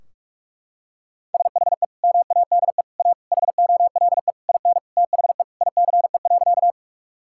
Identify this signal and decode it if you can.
Morse code, heard as a keyed tone, about 42 wpm, 700 Hz. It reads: SHE MADE A HOLE IN THE ICE.
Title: Sentences from Top 1600 Words 45wpm